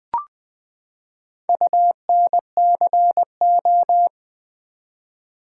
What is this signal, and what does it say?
Morse code, heard as a keyed tone, about 20 wpm, 700 Hz. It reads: UNCO
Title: Build A Word - 4 Letters At A Time - 11-Letter Words - 20wpm